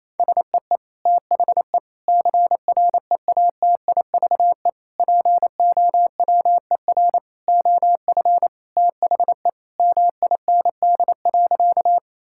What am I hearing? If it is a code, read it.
Morse code, heard as a keyed tone, about 28 wpm, 700 Hz. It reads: SEE THE CREATIVE POWER OF THE MIND.
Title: Sentences from Top 1000 Words - Review Part 5 - 28wpm